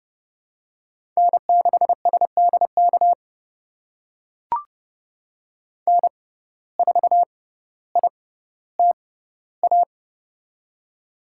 Morse code, heard as a keyed tone, about 30 wpm, spaced as for 15 wpm, 700 Hz. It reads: N6SDK N4ITA
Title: USA Call Signs - 5 Characters - 6x Character Spacing - 30wpm